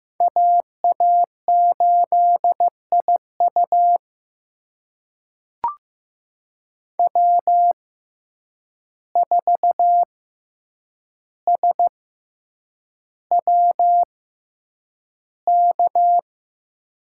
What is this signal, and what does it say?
Morse code, heard as a keyed tone, about 15 wpm, spaced as for 8 wpm, 700 Hz. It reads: AA8IU W4SWK
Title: USA Call Signs - 5 Characters - 6x Character Spacing - 15wpm